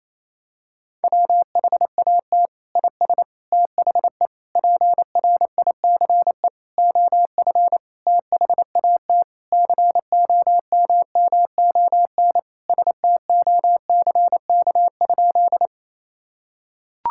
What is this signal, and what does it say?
Morse code, heard as a keyed tone, about 28 wpm, 700 Hz. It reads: WHAT IS THE PRICE OF THAT COMMON STOCK?
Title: Sentences from Top 1000 Words - Review Part 5 - 28wpm